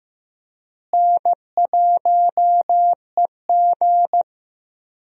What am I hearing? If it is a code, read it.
Morse code, heard as a keyed tone, about 15 wpm, 700 Hz. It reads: N1EG
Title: USA Call Signs - 4 Characters - Repeated 3x - 15wpm